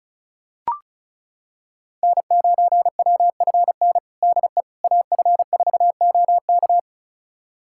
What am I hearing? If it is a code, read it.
Morse code, heard as a keyed tone, about 36 wpm, 700 Hz. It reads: N9WFN DE AF4OK
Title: USA Callsign DE Callsign - 5 Characters - 35wpm